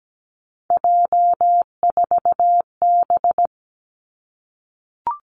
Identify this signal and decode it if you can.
Morse code, heard as a keyed tone, about 17 wpm, 700 Hz. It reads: J4B
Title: Three Challenging Letter-Number Combinations 17wpm